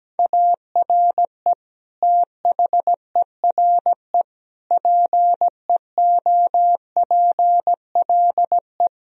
Morse code, heard as a keyed tone, about 17 wpm, 700 Hz. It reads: ARE THERE PEOPLE
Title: Sentences from Top 1000 Words - Review Part 3 - 17wpm